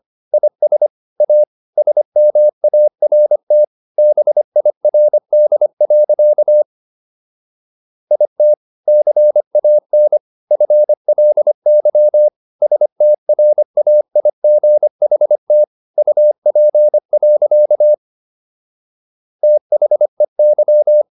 Morse code, heard as a keyed tone, about 25 wpm, 600 Hz. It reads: IS A SMART BIRD. IT CAN FLY STRAIGHT UP. THEY